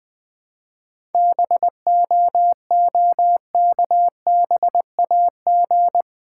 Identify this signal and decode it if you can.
Morse code, heard as a keyed tone, about 20 wpm, 700 Hz. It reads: BOOKBAG
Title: Easy Words with Hints - 20wpm